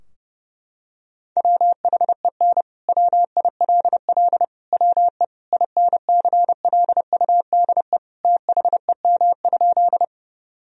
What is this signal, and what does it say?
Morse code, heard as a keyed tone, about 30 wpm, 700 Hz. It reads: WHEN WILL WE INCLUDE THEM?
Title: Sentences from Top 1000 Words - Review Part 3 - 30wpm